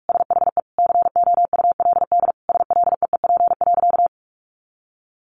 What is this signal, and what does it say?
Morse code, heard as a keyed tone, about 44 wpm, 700 Hz. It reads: SHE COULD SLEEP.
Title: Sentences from Top 500 Words 45wpm